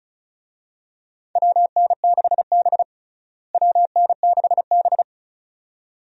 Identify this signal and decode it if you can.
Morse code, heard as a keyed tone, about 36 wpm, 700 Hz. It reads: WN6B WN6B
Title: USA Call Signs - 4 Characters - Repeated 2x - 35wpm